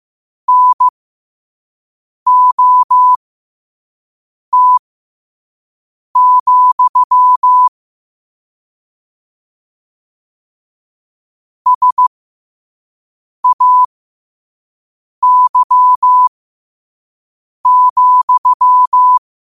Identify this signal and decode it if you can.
Morse code, heard as a keyed tone, about 15 wpm, spaced as for 5 wpm, 1000 Hz. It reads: NOT, SAY,
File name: 5wpm